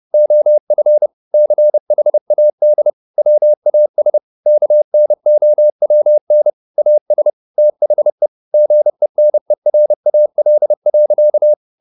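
Morse code, heard as a keyed tone, about 30 wpm, 600 Hz. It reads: OF CHAD WAS KNOWN AS THE GENERAL.
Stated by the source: Morse code at 30 wpm, the novel Never by Ken Follett in English, from CW med Gnister.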